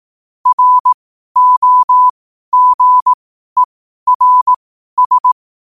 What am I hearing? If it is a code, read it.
Morse code, heard as a keyed tone, about 18 wpm, spaced as for 12 wpm, 1000 Hz. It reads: ROGERS
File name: qotdfiles / 12wpm